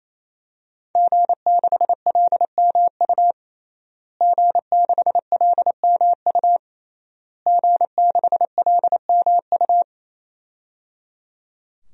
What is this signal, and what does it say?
Morse code, heard as a keyed tone, about 28 wpm, 700 Hz. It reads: G6LMU G6LMU G6LMU
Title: DXCC-100 Call Signs - 5 Characters - Repeated 3x - 28wpm